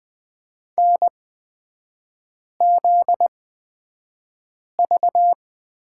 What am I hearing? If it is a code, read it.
Morse code, heard as a keyed tone, about 20 wpm, spaced as for 5 wpm, 700 Hz. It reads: NZV
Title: 250 Lesson 40e - BK - Three Characters 20wpm